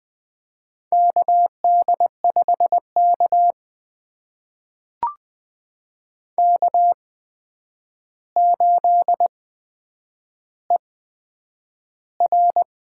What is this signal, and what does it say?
Morse code, heard as a keyed tone, about 20 wpm, spaced as for 9 wpm, 700 Hz. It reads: KD5K K8ER
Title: USA Call Signs - 4 Characters - 8x Character Spacing - 20wpm